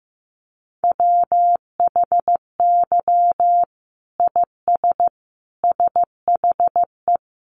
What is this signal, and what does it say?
Morse code, heard as a keyed tone, about 15 wpm, 700 Hz. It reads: WHY IS SHE
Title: Sentences from Top 1200 Words 15wpm